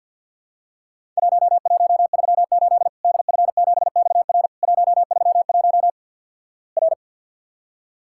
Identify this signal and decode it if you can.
Morse code, heard as a keyed tone, about 50 wpm, 700 Hz. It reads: 1138 DF7XR 131 R
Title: Worked All Europe DX Contest - QTC Exchange - 50wpm